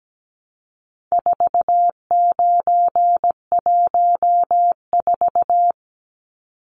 Morse code, heard as a keyed tone, about 17 wpm, 700 Hz. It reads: 4914
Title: Numbers - 4-Digit 17wpm